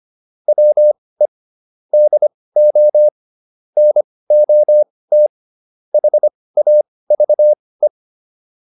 Morse code, heard as a keyed tone, about 25 wpm, spaced as for 18 wpm, 600 Hz. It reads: WE DO NOT HAVE
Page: intermediate-practice-files